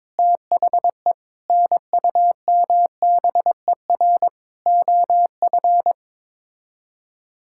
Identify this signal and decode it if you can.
Morse code, heard as a keyed tone, about 22 wpm, 700 Hz. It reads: THE NUMBER OF
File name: practice / 22wpm / Sets of 6 Words - 22wpm